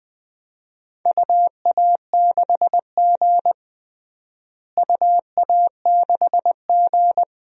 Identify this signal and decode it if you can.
Morse code, heard as a keyed tone, about 21 wpm, 700 Hz. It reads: UA6G UA6G
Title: DXCC-100 Call Signs - 4 Characters - Repeated 2x - 20wpm